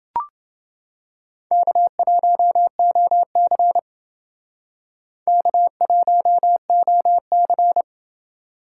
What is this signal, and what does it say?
Morse code, heard as a keyed tone, about 28 wpm, 700 Hz. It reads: K1OC K1OC